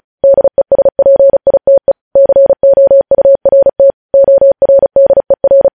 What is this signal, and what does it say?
Morse code, heard as a keyed tone, about 36 wpm, 550 Hz. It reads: DESPITE COURT ORDER
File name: podcast / News headlines-1x-35x35